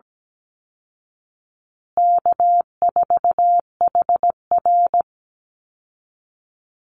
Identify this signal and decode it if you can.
Morse code, heard as a keyed tone, about 17 wpm, 700 Hz. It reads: K4HR
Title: Easy Call Signs Repeated 3x - 17wpm